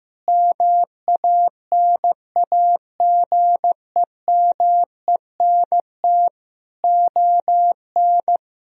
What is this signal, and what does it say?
Morse code, heard as a keyed tone, about 15 wpm, 700 Hz. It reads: MANAGEMENT ON
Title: Sentences from Top 1000 Words - Review Part 4 - 15wpm